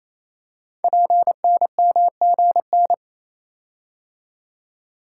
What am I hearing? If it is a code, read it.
Morse code, heard as a keyed tone, about 29 wpm, 700 Hz. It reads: PNMGN
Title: Code Groups - 5 Letters - 28wpm